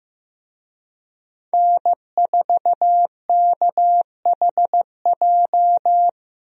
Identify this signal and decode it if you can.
Morse code, heard as a keyed tone, about 15 wpm, 700 Hz. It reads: N4KHJ
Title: USA Call Signs - 5 Characters - Repeated 2x - 15wpm